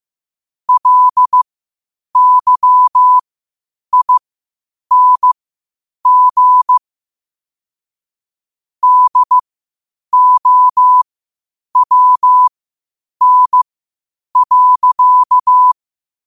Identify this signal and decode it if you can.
Morse code, heard as a keyed tone, about 15 wpm, spaced as for 8 wpm, 1000 Hz. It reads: LYING DOWN.
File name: qotdfiles / 8wpm